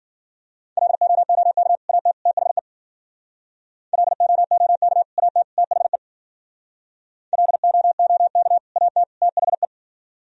Morse code, heard as a keyed tone, about 49 wpm, 700 Hz. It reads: LOOK AT THE LOOK AT THE LOOK AT THE
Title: Speed-Racing - Sets of 3 Words 40wpm